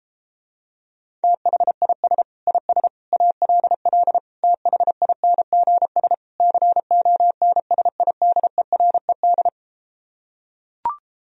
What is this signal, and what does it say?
Morse code, heard as a keyed tone, about 33 wpm, 700 Hz. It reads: THIS IS ALL THINGS CONSIDERED